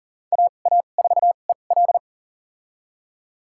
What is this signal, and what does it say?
Morse code, heard as a keyed tone, about 40 wpm, spaced as for 29 wpm, 700 Hz. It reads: AA4EL
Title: USA Call Signs - 2x Character Spacing - 40wpm